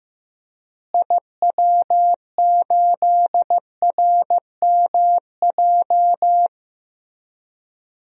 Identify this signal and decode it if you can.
Morse code, heard as a keyed tone, about 15 wpm, 700 Hz. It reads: IW8RMJ
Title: DXCC-100 Call Signs - 6 Characters - Repeated 2x - 15wpm